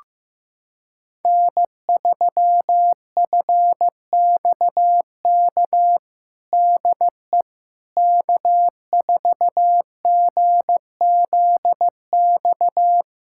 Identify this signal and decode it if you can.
Morse code, heard as a keyed tone, about 15 wpm, 700 Hz. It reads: N3FXK DE K4GZX
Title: USA Callsign DE Callsign - 5 Characters - 15wpm